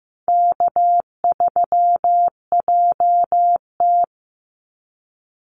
Morse code, heard as a keyed tone, about 15 wpm, 700 Hz. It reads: K3JT
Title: Easy Call Signs - 15wpm